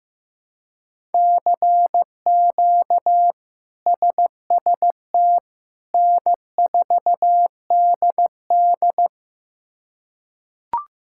Morse code, heard as a keyed tone, about 15 wpm, 700 Hz. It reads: CQ SST N4DD
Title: K1USN SST QSO Practice - 15wpm